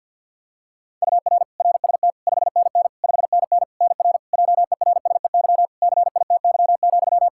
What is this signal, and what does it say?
Morse code, heard as a keyed tone, about 52 wpm, 700 Hz. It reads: UR RST 5NN 5NN NR JERSEY CITY,
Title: Practice QSO - Exchange 1 - Beginning - 50wpm